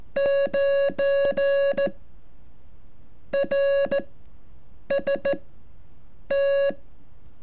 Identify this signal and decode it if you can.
Morse code, heard as a keyed tone, about 14 wpm, spaced as for 7 wpm, 560 Hz. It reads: 9 RST
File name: curso / ejercicio 33